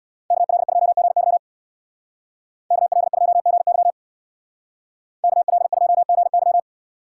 Morse code, heard as a keyed tone, about 64 wpm, 700 Hz. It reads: KL2GY KL2GY KL2GY